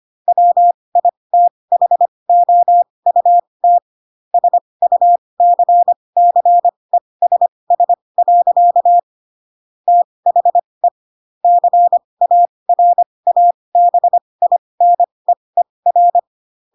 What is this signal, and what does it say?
Morse code, heard as a keyed tone, about 25 wpm, spaced as for 20 wpm, 700 Hz. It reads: WITHOUT SUCCESS. THE CARABINEER